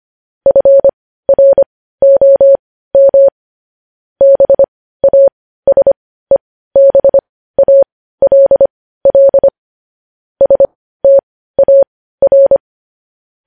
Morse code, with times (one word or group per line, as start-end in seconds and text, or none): 0.5-3.3 FROM
4.2-9.5 BASEBALL
10.4-12.6 STAR